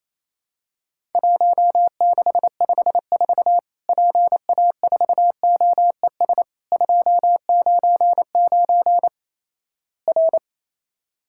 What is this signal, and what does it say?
Morse code, heard as a keyed tone, about 28 wpm, 700 Hz. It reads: 1654 PA4OES 299 R
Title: Worked All Europe DX Contest - QTC Exchange - 28wpm